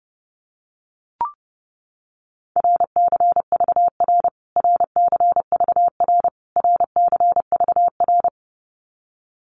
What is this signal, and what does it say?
Morse code, heard as a keyed tone, about 30 wpm, 700 Hz. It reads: RCVR RCVR RCVR
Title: Abbreviations Repeated 3x - 30wpm